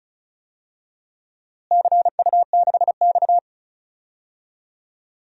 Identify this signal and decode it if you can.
Morse code, heard as a keyed tone, about 36 wpm, 700 Hz. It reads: CU6X
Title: DXCC-100 Call Signs - 4 Characters - Repeated 3x - 35wpm